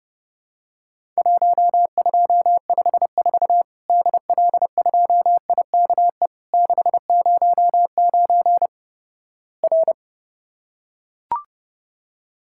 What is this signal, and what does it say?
Morse code, heard as a keyed tone, about 30 wpm, 700 Hz. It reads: 1254 DL2IKE 609 R
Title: Worked All Europe DX Contest - QTC Exchange - 30wpm